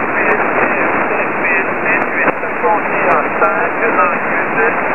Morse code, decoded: none